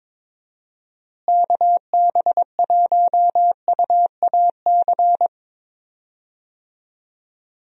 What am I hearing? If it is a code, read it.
Morse code, heard as a keyed tone, about 21 wpm, 700 Hz. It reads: KB1UAC
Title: USA Call Signs - 6 Characters - Repeated 3x - 22wpm